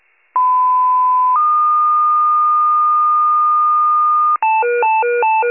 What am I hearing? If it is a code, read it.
Two-tone page, heard as a one-second tone, then a three-second tone, about 1010 then 1220 Hz. It is heard from 0.4 to 4.4 s.
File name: pager / 94 RESCUE OFC / wcalert